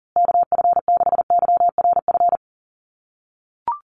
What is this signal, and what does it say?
Morse code, heard as a keyed tone, about 40 wpm, 700 Hz. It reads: KF6YRF
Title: USA Call Signs Repeated 2x 40wpm